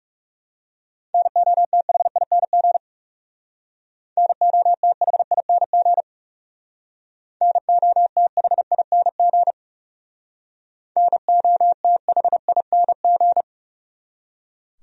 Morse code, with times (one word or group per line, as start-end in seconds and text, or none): 1.1-2.8 NOTHING
4.2-6.0 NOTHING
7.4-9.5 NOTHING
11.0-13.4 NOTHING